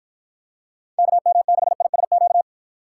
Morse code, heard as a keyed tone, about 49 wpm, 700 Hz. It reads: KM6ISQ